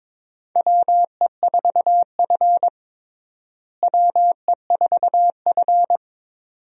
Morse code, heard as a keyed tone, about 22 wpm, 700 Hz. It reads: WE4F WE4F
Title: USA Call Signs - 4 Characters - Repeated 2x - 22wpm